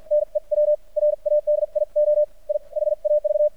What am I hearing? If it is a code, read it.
Morse code, heard as a keyed tone, about 52 wpm, spaced as for 42 wpm, 600 Hz. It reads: NEW RADIO I HAV
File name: archived / CRSnet